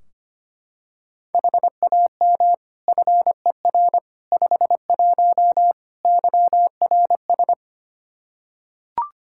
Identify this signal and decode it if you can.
Morse code, heard as a keyed tone, about 25 wpm, 700 Hz. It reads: HAM FER 51 YRS
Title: Practice QSO - Exchange 3 - More Info - 25wpm